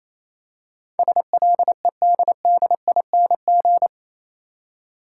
Morse code, heard as a keyed tone, about 27 wpm, 700 Hz. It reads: SLEDDING
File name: Easy Words - Speed-Racing with Hints - 22wpm